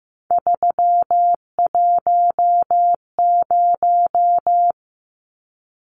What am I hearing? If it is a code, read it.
Morse code, heard as a keyed tone, about 15 wpm, 700 Hz. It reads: 310
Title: Numbers - 3-Digit 15wpm